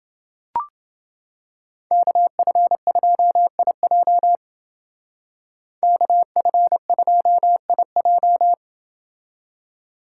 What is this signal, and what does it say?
Morse code, heard as a keyed tone, about 28 wpm, 700 Hz. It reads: KF2IJ KF2IJ